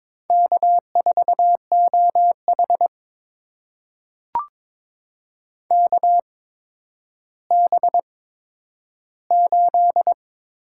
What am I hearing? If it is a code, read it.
Morse code, heard as a keyed tone, about 22 wpm, spaced as for 11 wpm, 700 Hz. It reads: K4OH KB8
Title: USA Call Signs - 8x Character Spacing - 22wpm